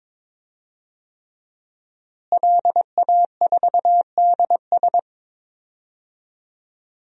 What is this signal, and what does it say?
Morse code, heard as a keyed tone, about 21 wpm, 700 Hz. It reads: LA4DS
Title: DXCC-100 Call Signs - 5 Characters - Repeated 3x - 22wpm